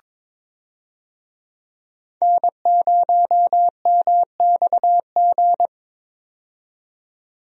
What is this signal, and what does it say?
Morse code, heard as a keyed tone, about 22 wpm, 700 Hz. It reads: N0MXG
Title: USA Call Signs - 5 Characters - 22wpm